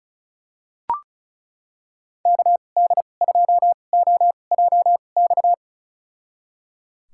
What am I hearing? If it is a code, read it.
Morse code, heard as a keyed tone, about 34 wpm, spaced as for 25 wpm, 700 Hz. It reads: KD2OJX
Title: USA Call Signs - 6 Characters - 2x Character Spacing - 35wpm